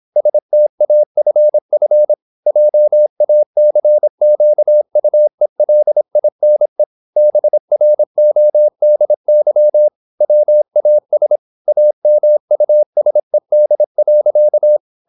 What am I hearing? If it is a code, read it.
Morse code, heard as a keyed tone, about 26 wpm, 600 Hz. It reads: STAFF JACQUELINE BRODY WAS AMUSED.
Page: Never 26wpm | CW med Gnister